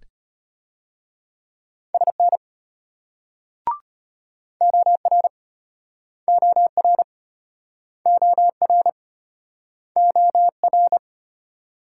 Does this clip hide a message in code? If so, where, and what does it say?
Yes: Morse code from 1.9 to 11.0 s, heard as a keyed tone, about 35 wpm, 700 Hz. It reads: SN OR OR OR OR